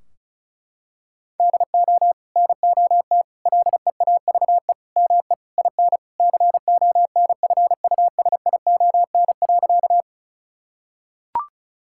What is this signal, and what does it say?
Morse code, heard as a keyed tone, about 36 wpm, 700 Hz. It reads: DO NOT LEAVE ME IN CONFUSION.
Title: Sentences from Top 1600 Words 35wpm